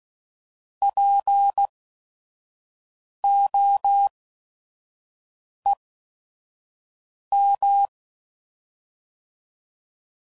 Morse code, heard as a keyed tone, about 16 wpm, spaced as for 5 wpm, 800 Hz. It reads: POEM
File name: Prefixes / By First Letter / Ps